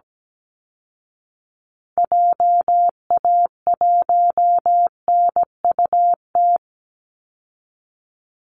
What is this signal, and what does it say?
Morse code, heard as a keyed tone, about 17 wpm, 700 Hz. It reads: JA1NUT
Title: Call Signs Repeated 2x - 17wpm